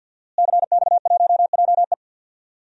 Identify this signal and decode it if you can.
Morse code, heard as a keyed tone, about 49 wpm, 700 Hz. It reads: CX1JE
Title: DXCC-100 Call Signs - 5 Characters - Repeated 3x - 50wpm